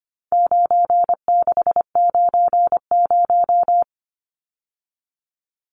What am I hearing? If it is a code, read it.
Morse code, heard as a keyed tone, about 25 wpm, 700 Hz. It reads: 9690
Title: Numbers - 4-Digit 25wpm